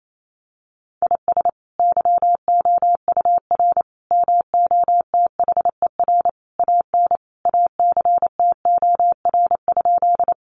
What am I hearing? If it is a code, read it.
Morse code, heard as a keyed tone, about 28 wpm, 700 Hz. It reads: IS YOUR MOTHER AN ACTOR?
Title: Sentences from Top 1500 Words 28wpm